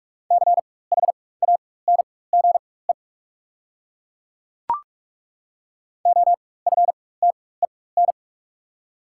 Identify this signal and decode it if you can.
Morse code, heard as a keyed tone, about 43 wpm, spaced as for 21 wpm, 690 Hz. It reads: CHANGE OFTEN
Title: 4x Character Spacing - Top 300 Words - 45wpm